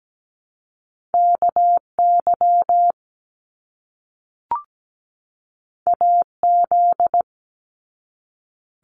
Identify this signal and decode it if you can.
Morse code, heard as a keyed tone, about 17 wpm, 700 Hz. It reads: KY AZ
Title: US States 17wpm